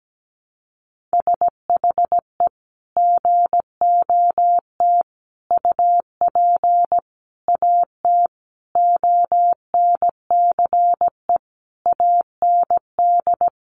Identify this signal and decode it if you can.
Morse code, heard as a keyed tone, about 17 wpm, 700 Hz. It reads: SHE GOT UP AT ONCE AND
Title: Sentences from Top 1100 Words 17wpm